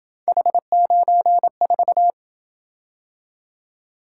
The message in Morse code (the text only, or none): H94